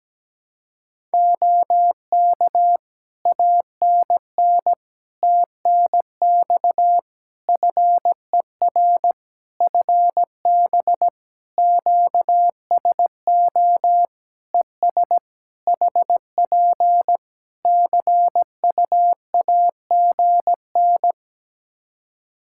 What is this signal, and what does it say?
Morse code, heard as a keyed tone, about 17 wpm, 700 Hz. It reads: OK ANN TNX FER FB QSO ES HP CUAGN
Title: K1USN SST QSO Practice 17wpm